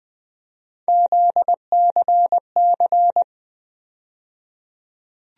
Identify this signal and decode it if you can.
Morse code, heard as a keyed tone, about 20 wpm, 700 Hz. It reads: ZCC
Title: USA Call Sign Postfixes - 3 Letters 20wpm